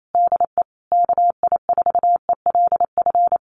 Morse code, heard as a keyed tone, about 28 wpm, 700 Hz. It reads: DE KI4ELF
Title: Common QSO Phrases Encore 28wpm